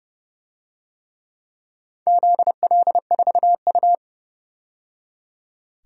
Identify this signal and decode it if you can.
Morse code, heard as a keyed tone, about 30 wpm, 700 Hz. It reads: ZL4U
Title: DXCC-100 Call Signs - 4 Characters - 30wpm